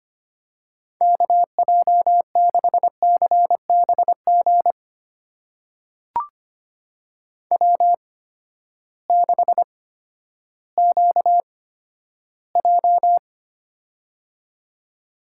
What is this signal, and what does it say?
Morse code, heard as a keyed tone, about 25 wpm, spaced as for 12 wpm, 700 Hz. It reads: KJ6CBG W6QJ
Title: USA Call Signs - 8x Character Spacing - 25wpm